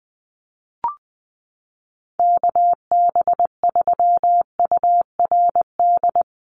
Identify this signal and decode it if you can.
Morse code, heard as a keyed tone, about 20 wpm, 700 Hz. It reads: KB3URD
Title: USA Call Signs 20wpm